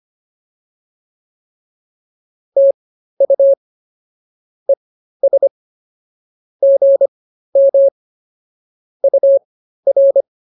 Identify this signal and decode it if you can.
Morse code, heard as a keyed tone, about 24 wpm, spaced as for 13 wpm, 550 Hz. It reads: TU ES GM UR